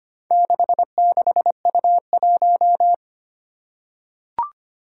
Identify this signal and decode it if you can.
Morse code, heard as a keyed tone, about 25 wpm, 700 Hz. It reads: 66U1
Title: Four Challenging Letter-Number Combinations 25wpm